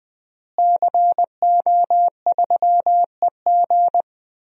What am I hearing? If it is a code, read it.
Morse code, heard as a keyed tone, about 20 wpm, 700 Hz. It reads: CO3EG
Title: DXCC-100 Call Signs - 5 Characters - Repeated 2x - 20wpm